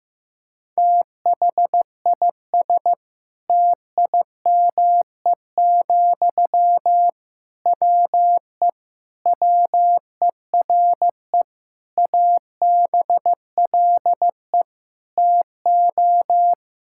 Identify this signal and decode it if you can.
Morse code, heard as a keyed tone, about 14 wpm, 700 Hz. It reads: THIS TIME, WE WERE ABLE TO
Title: Sentences from Top 1000 Words - Review Part 6 - 15wpm